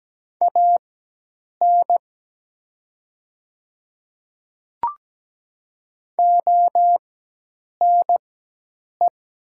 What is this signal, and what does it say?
Morse code, heard as a keyed tone, about 17 wpm, spaced as for 8 wpm, 700 Hz. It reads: AN ONE
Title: 4x Character Spacing - Top 100 Words - 17wpm